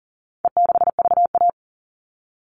Morse code, heard as a keyed tone, about 40 wpm, 700 Hz. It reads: E6VA
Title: All Cut Numbers - 4-Digit 40wpm